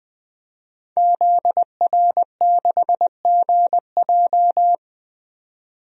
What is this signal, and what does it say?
Morse code, heard as a keyed tone, about 20 wpm, 700 Hz. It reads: ZR6GJ
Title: DXCC-100 Call Signs - 5 Characters - 20wpm